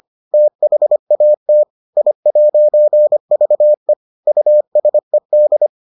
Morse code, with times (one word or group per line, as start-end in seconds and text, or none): 0.3-1.6 THAT
2.0-3.9 I'VE
4.3-5.7 USED